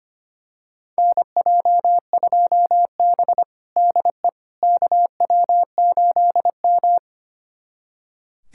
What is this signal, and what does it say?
Morse code, heard as a keyed tone, about 25 wpm, 700 Hz. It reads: NJ2B DE KW8M
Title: USA Callsign DE Callsign - 4 Characters - 25wpm